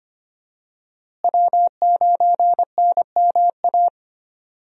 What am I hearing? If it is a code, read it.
Morse code, heard as a keyed tone, about 25 wpm, 700 Hz. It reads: W9NMA
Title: USA Call Signs - 5 Characters - Repeated 2x - 25wpm